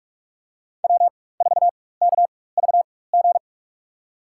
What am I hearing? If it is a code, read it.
Morse code, heard as a keyed tone, about 47 wpm, spaced as for 21 wpm, 700 Hz. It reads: W4KVG
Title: USA Call Signs - 4x Character Spacing - 45wpm